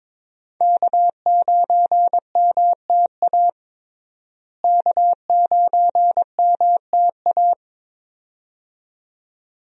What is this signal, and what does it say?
Morse code, heard as a keyed tone, about 22 wpm, 700 Hz. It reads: K9MTA K9MTA
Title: USA Call Signs - 5 Characters - Repeated 2x - 22wpm